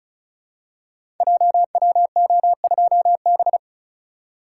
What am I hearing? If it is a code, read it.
Morse code, heard as a keyed tone, about 36 wpm, 700 Hz. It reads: JWO2B
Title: Code Groups - 5 Letter:Numbers - 35wpm